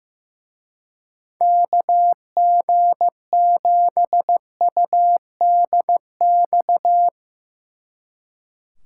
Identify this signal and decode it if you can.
Morse code, heard as a keyed tone, about 17 wpm, 700 Hz. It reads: KG7UDX
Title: Speed Racing - USA 2x3 Call Signs 15wpm